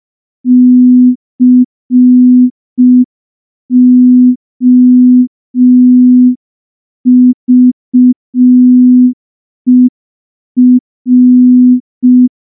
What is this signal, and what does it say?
Morse code, heard as a keyed tone, about 5 wpm, 250 Hz. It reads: COVER
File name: morse / morse-training-05WPM